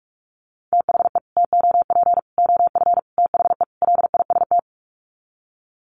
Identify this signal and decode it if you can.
Morse code, heard as a keyed tone, about 44 wpm, 700 Hz. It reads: THE TOP OF THE LIST
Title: Sets of 5 Words Encore2 45wpm